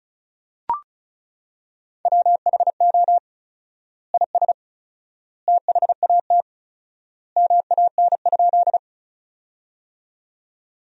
Morse code, heard as a keyed tone, about 36 wpm, 700 Hz. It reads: WHO IS THAT MAN?
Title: 4x Word-Spacing - Sentences from Top 200 Words - 35wpm